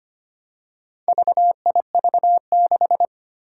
Morse code, heard as a keyed tone, about 25 wpm, 700 Hz. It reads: VIV6
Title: Four Challenging Letter-Number Combinations 25wpm